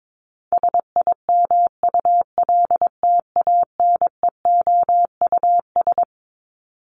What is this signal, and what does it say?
Morse code, heard as a keyed tone, about 22 wpm, 700 Hz. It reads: SIMULTANEOUS
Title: LY Suffix - Top 100 Words with Base Words 22wpm